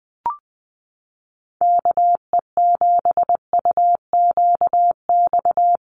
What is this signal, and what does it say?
Morse code, heard as a keyed tone, about 20 wpm, 700 Hz. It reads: KE7UQX
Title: USA Call Signs Repeated 2x 20wpm